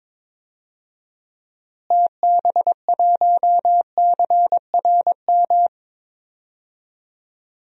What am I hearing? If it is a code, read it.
Morse code, heard as a keyed tone, about 22 wpm, 700 Hz. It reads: TB1CRM
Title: DXCC-100 Call Signs - 6 Characters - Repeated 3x - 22wpm